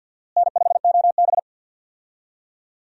Morse code, heard as a keyed tone, about 50 wpm, 700 Hz. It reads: N5OB
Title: USA Call Signs - 4 Characters - Repeated 2x - 50wpm